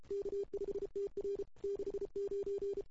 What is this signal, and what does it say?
Morse code, heard as a keyed tone, about 32 wpm, 400 Hz. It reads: K5TR 69
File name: CW / 2008-ss-cw-k5tr-op-n5rz -00:00